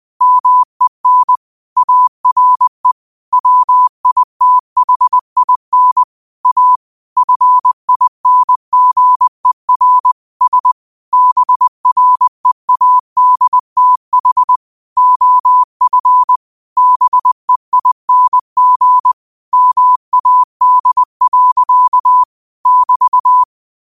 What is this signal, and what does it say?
Morse code, heard as a keyed tone, about 20 wpm, 1000 Hz. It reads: MEN ARE WITHIN A FINGER S BREADTH OF BEING MAD. <BT>
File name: qotdfiles / 20wpm